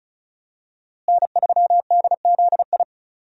Morse code, heard as a keyed tone, about 34 wpm, 700 Hz. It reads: N3DZI
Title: USA Call Signs - 5 Characters - Repeated 2x - 35wpm